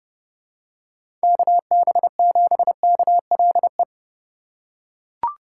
Morse code, heard as a keyed tone, about 30 wpm, 700 Hz. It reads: KB7KLE
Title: USA Call Signs - 4x Character Spacing - 30wpm